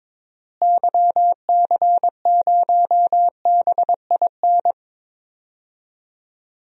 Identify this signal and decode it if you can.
Morse code, heard as a keyed tone, about 22 wpm, 700 Hz. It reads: YC0BIN
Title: DXCC-100 Call Signs - 6 Characters - Repeated 3x - 22wpm